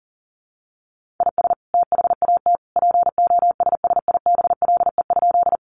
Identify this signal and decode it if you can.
Morse code, heard as a keyed tone, about 40 wpm, 700 Hz. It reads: IS THAT POSSIBLE?
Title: Sentences from Top 500 Words 40wpm